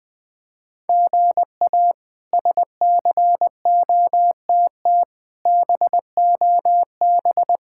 Morse code, heard as a keyed tone, about 20 wpm, 700 Hz. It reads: GA SCOTT BOB
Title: K1USN SST QSO Practice - 20wpm